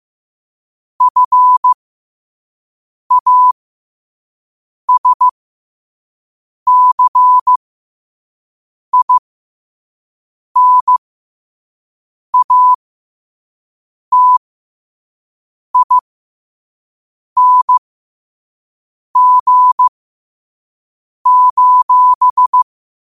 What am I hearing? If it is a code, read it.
Morse code, heard as a keyed tone, about 15 wpm, spaced as for 5 wpm, 1000 Hz. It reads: FASCINATING:
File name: qotdfiles / 5wpm